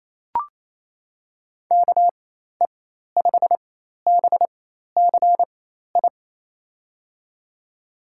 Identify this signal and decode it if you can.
Morse code, heard as a keyed tone, about 28 wpm, spaced as for 13 wpm, 700 Hz. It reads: KE5BCI
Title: USA Call Signs - 6 Characters - 4x Character Spacing - 28wpm